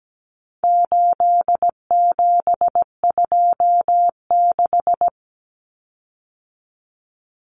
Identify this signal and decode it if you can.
Morse code, heard as a keyed tone, about 17 wpm, 700 Hz. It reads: 8726
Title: Common Cut Numbers - 4-Digit 17wpm